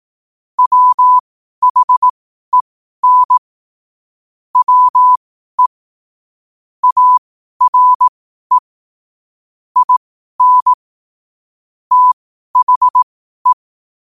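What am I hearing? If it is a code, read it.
Morse code, heard as a keyed tone, about 18 wpm, spaced as for 12 wpm, 1000 Hz. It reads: WHEN WE ARE IN THE
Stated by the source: Morse code quote-of-the-day practice at 12 words per minute.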